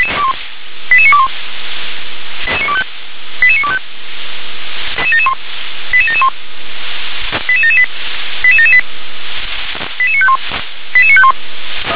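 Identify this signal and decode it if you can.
Five-tone selective call (ZVEI), heard as quick runs of five tones, tones of about 70 ms. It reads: 88921 88925 88771 88778 88778 88951 88951
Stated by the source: ZVEI five-tone call 88083 received on SDR.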